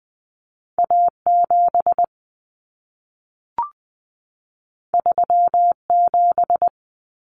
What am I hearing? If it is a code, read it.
Morse code, heard as a keyed tone, about 20 wpm, 700 Hz. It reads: A7 37
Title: Common Cut Numbers - 2-Digit 20wpm